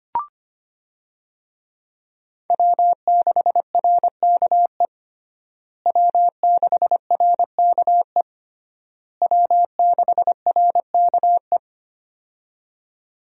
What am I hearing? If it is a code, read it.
Morse code, heard as a keyed tone, about 25 wpm, 700 Hz. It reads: W6RKE W6RKE W6RKE
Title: USA Call Signs - 5 Characters - Repeated 3x - 25wpm